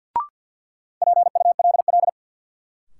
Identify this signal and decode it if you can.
Morse code, heard as a keyed tone, about 53 wpm, 700 Hz. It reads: PULL
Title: Sentences from Top 400 Words - New Word Drill - 50wpm